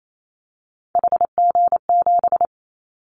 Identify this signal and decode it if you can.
Morse code, heard as a keyed tone, about 28 wpm, 700 Hz. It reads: HG7
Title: Callsign Prefixes for Top 50 DXCC - 28wpm